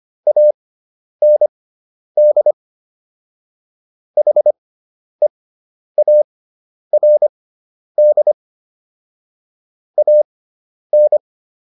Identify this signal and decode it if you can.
Morse code, heard as a keyed tone, about 25 wpm, spaced as for 10 wpm, 600 Hz. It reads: AND HEARD AN